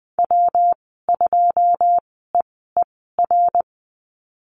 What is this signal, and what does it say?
Morse code, heard as a keyed tone, about 20 wpm, spaced as for 14 wpm, 700 Hz. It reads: W2EER
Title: 122 Lesson 20g - Number 3 - Call Signs 20wpm